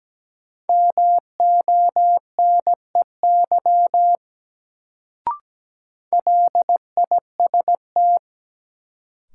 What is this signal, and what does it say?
Morse code, heard as a keyed tone, about 17 wpm, 700 Hz. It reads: MONEY LIST
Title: Sentences from Top 400 Words - New Word Drill - 17wpm